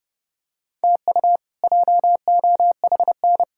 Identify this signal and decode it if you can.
Morse code, heard as a keyed tone, about 30 wpm, 700 Hz. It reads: TU JOHN
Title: K1USN SST QSO Practice - 30wpm